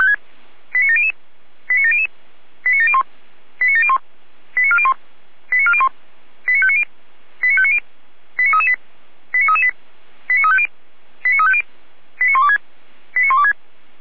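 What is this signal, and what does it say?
Five-tone selective call (ZVEI), heard as quick runs of five tones, tones of about 70 ms. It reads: 89700 89700 89701 89701 89401 89401 89590 89590 89338 89338 89350 89350 89126 89126
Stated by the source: ZVEI five-tone call received by SDR, feuerwehr-orlamuende.org fire brigade audio archive, code 89235.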